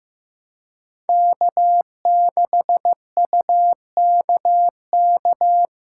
Morse code, heard as a keyed tone, about 15 wpm, 700 Hz. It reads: K6UKK
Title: USA Call Signs - 5 Characters - 2x Character Spacing - 15wpm